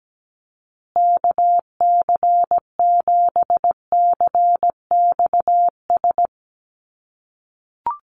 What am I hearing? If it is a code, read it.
Morse code, heard as a keyed tone, about 17 wpm, 700 Hz. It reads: KC7CXS
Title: USA Call Signs Repeated 3x 17wpm